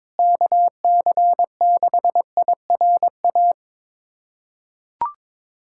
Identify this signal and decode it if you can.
Morse code, heard as a keyed tone, about 22 wpm, 700 Hz. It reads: KC6IRA
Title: USA Callsign DE Callsign 22wpm